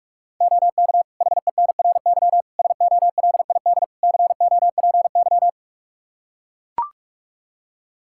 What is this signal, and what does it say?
Morse code, heard as a keyed tone, about 41 wpm, 700 Hz. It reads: OK HENRY SOLID COPY
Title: Practice QSO - Exchange 3 - More Info - 45wpm